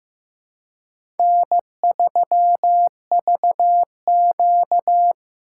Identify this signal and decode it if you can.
Morse code, heard as a keyed tone, about 15 wpm, 700 Hz. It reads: N3VQ
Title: USA Call Signs - 4 Characters - 4x Character Spacing - 15wpm